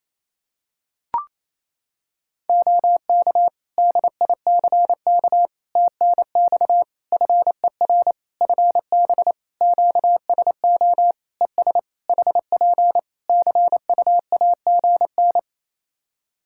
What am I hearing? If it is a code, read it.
Morse code, heard as a keyed tone, about 28 wpm, 700 Hz. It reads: OK DICK TNX FER FB QSO ES HP CUAGN
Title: Practice QSO - Exchange 4 - Ending - 28wpm